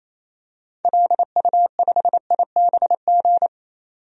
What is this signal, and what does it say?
Morse code, heard as a keyed tone, about 28 wpm, 700 Hz. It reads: LU5IBG
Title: DXCC-100 Call Signs - 6 Characters - 28wpm